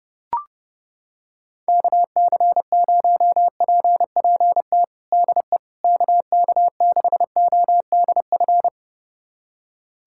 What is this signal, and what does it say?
Morse code, heard as a keyed tone, about 32 wpm, 700 Hz. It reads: KC0PPT DE KK6ODF
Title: USA Callsign DE Callsign 30wpm